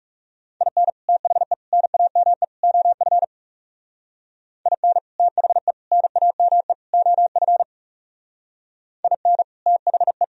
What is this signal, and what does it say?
Morse code, heard as a keyed tone, about 37 wpm, 700 Hz. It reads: IN THE NAME OF IN THE NAME OF IN THE
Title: Speed-Racing - Sets of 4 Words 30wpm